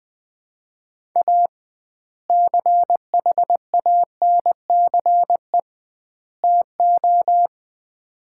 Morse code, heard as a keyed tone, about 21 wpm, 700 Hz. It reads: A CHANCE TO
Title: 2x Word-Spacing - Sets of 3 Words - 20wpm